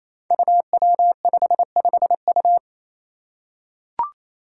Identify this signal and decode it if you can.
Morse code, heard as a keyed tone, about 28 wpm, 700 Hz. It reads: UW55U
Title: Five Challenging Letter-Number Combinations 28wpm